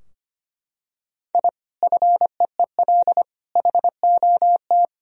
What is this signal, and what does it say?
Morse code, heard as a keyed tone, about 25 wpm, 700 Hz. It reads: I FEEL HOT
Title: Sentences from Top 1000 Words - Review Part 3 - 25wpm